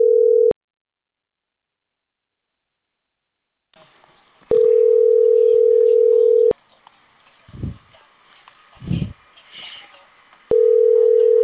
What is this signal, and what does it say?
Ringback tone (North America), 440 + 480 Hz.